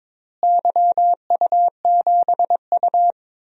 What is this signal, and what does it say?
Morse code, heard as a keyed tone, about 22 wpm, 700 Hz. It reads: YU7U
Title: DXCC-100 Call Signs - 4 Characters - Repeated 2x - 22wpm